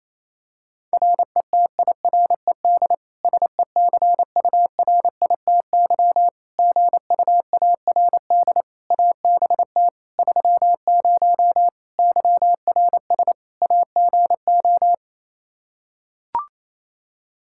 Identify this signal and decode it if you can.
Morse code, heard as a keyed tone, about 27 wpm, 700 Hz. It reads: RETIRED SECURITY GUARD ABT 30 YRS AGO
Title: K1USN SST QSO Practice 28wpm